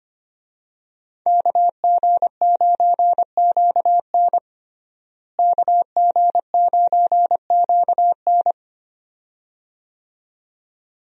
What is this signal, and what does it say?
Morse code, heard as a keyed tone, about 25 wpm, 700 Hz. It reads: KG9QN KG9QN
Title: USA Call Signs - 5 Characters - Repeated 2x - 25wpm